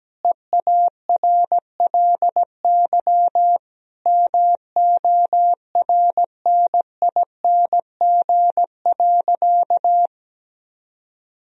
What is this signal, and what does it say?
Morse code, heard as a keyed tone, about 17 wpm, 700 Hz. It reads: EARLY MORNING.
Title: Sentences from Top 1000 Words - Review Part 3 - 17wpm